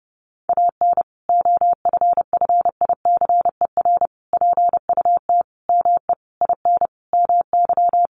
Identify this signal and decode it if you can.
Morse code, heard as a keyed tone, about 30 wpm, 700 Hz. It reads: AN OFFICER PUT ME IN MY
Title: Sentences from Top 1200 Words 30wpm